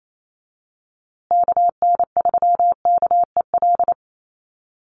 Morse code, heard as a keyed tone, about 28 wpm, 700 Hz. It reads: KN3KEL
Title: USA Call Signs Repeated 3x 28wpm